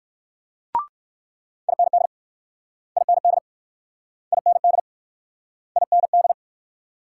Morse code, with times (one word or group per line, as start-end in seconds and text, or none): 1.7-2.1 IND
3.0-3.4 IND
4.3-4.8 IND
5.8-6.3 IND